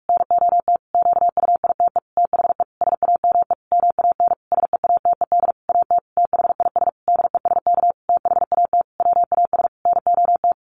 Morse code, heard as a keyed tone, about 44 wpm, 700 Hz. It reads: NOT QUITE THE SAME MAN SEATED AT THIS DESK THAT WAS NOT
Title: A Sound Of Thunder 45wpm